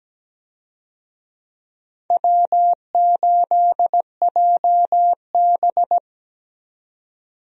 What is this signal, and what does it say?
Morse code, heard as a keyed tone, about 17 wpm, 700 Hz. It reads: W8JB
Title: USA Call Signs - 4 Characters - 17wpm